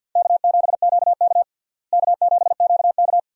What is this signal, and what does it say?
Morse code, heard as a keyed tone, about 50 wpm, 700 Hz. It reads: K7QK K7QK
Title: USA Call Signs - 4 Characters - Repeated 2x - 50wpm